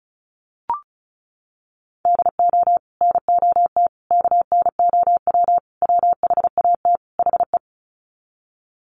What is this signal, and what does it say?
Morse code, heard as a keyed tone, about 34 wpm, 700 Hz. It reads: DO NOT KNOW WHAT HE
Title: Sets of 5 Words Encore2 35wpm